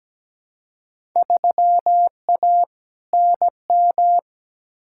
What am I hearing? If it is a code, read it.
Morse code, heard as a keyed tone, about 17 wpm, 700 Hz. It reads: 3A NM
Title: Field Day Exchanges Encore - 17wpm